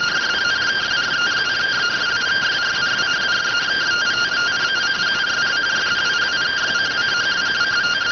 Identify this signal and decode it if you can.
Two-tone teleprinter signal, about 170 Hz shift.